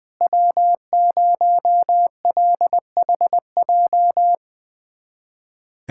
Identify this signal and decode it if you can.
Morse code, heard as a keyed tone, about 20 wpm, 700 Hz. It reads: W0LHJ
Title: USA Callsign DE Callsign - 5 Characters - 20wpm